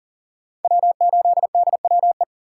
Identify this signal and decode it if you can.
Morse code, heard as a keyed tone, about 40 wpm, 700 Hz. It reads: W8DWE
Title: USA Call Signs - 2x Character Spacing - 40wpm